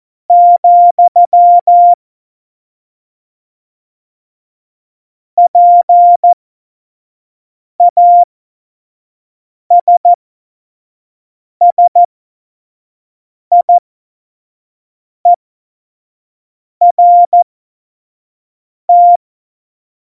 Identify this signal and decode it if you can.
Morse code, heard as a keyed tone, about 14 wpm, spaced as for 5 wpm, 700 Hz. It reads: , PASSIERT
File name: Morsekurs / Witze / Witz 29. Mai 2018, 25 F70